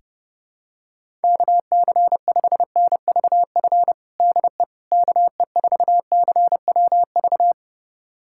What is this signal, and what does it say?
Morse code, heard as a keyed tone, about 30 wpm, 700 Hz. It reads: KC5NVF DE KE4CWV
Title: USA Callsign DE Callsign 30wpm